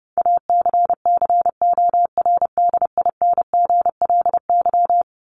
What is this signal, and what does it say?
Morse code, heard as a keyed tone, about 30 wpm, 700 Hz. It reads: ACCORDINGLY
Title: LY Suffix - Top 100 Words with Base Words 30wpm